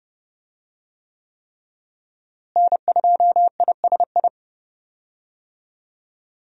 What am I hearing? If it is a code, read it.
Morse code, heard as a keyed tone, about 32 wpm, 700 Hz. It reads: N2ISI
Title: USA Call Signs - 5 Characters - Repeated 3x - 30wpm